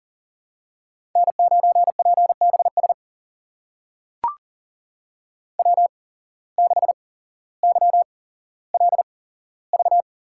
Morse code, heard as a keyed tone, about 42 wpm, spaced as for 17 wpm, 700 Hz. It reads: N9PBS W6YLV
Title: USA Call Signs - 5 Characters - 8x Character Spacing - 40wpm